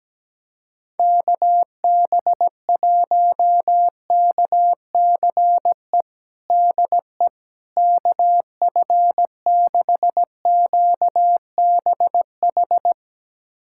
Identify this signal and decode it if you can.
Morse code, heard as a keyed tone, about 17 wpm, 700 Hz. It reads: KB1KCE DE KF6QBH
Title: USA Callsign DE Callsign 17wpm